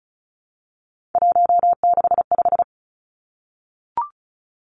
Morse code, heard as a keyed tone, about 36 wpm, 700 Hz. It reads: 165
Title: Three Challenging Letter-Number Combinations 35wpm